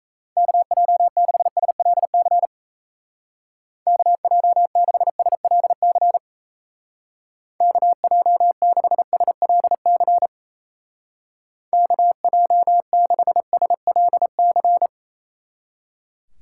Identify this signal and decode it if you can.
Morse code, heard as a keyed tone, about 34 wpm, 700 Hz. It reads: KJ6SLC KJ6SLC KJ6SLC KJ6SLC
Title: Speed Racing - USA 2x3 Call Signs 28wpm